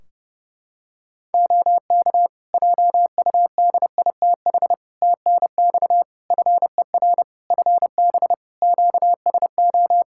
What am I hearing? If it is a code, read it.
Morse code, heard as a keyed tone, about 29 wpm, 700 Hz. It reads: OK JUDITH TNX FER FB QSO
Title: Practice QSO - Exchange 4 - Ending - 30wpm